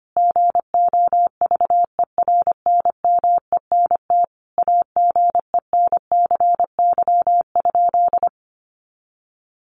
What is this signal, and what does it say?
Morse code, heard as a keyed tone, about 25 wpm, 700 Hz. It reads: GOVERNMENT AGENCY?
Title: Sentences from Top 1300 Words 25wpm